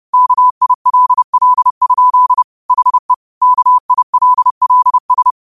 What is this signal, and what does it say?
Morse code, heard as a keyed tone, about 30 wpm, 1000 Hz. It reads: KILL? HE KILLS
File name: qotdfiles / 30wpm